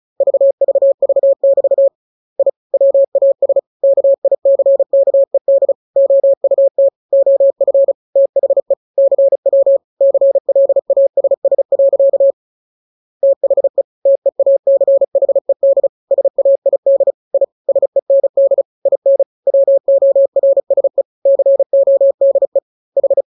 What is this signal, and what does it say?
Morse code, heard as a keyed tone, about 34 wpm, 550 Hz. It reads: VVV<BT> I WAS KICKED OUT OF THE CW CLASS. THE TEACHED SAID I SEND IN WORSE CODE H